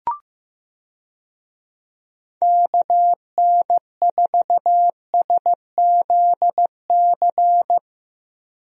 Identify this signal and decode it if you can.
Morse code, heard as a keyed tone, about 15 wpm, 700 Hz. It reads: KN4SZC
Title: USA Call Signs - 6 Characters - Repeated 2x - 15wpm